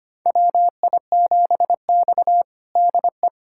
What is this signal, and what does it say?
Morse code, heard as a keyed tone, about 25 wpm, 700 Hz. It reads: WI7X DE
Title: USA Callsign DE Callsign - 4 Characters - 25wpm